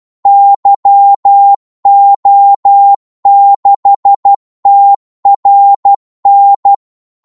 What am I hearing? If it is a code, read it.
Morse code, heard as a keyed tone, about 12 wpm, 800 Hz. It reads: YO6TRN
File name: Morse / qso-12wpm